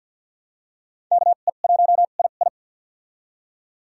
Morse code, heard as a keyed tone, about 49 wpm, spaced as for 36 wpm, 700 Hz. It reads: KE1II
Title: USA Call Signs - 5 Characters - 2x Character Spacing - 50wpm